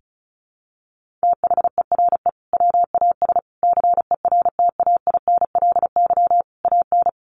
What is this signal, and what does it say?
Morse code, heard as a keyed tone, about 34 wpm, 700 Hz. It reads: THERE WAS CERTAINLY AN
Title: Sentences from Top 1400 Words 35wpm